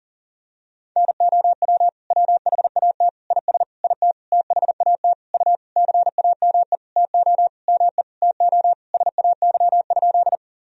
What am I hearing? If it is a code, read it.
Morse code, heard as a keyed tone, about 40 wpm, 700 Hz. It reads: NOW WHAT IS IT THAT U CAME TO ME TO SAY?
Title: Sentences from Top 200 Words and Abbreviations 40wpm